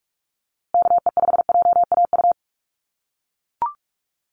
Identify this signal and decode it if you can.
Morse code, heard as a keyed tone, about 44 wpm, 690 Hz. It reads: KE5JAU
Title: USA Call Signs Repeated 2x 45wpm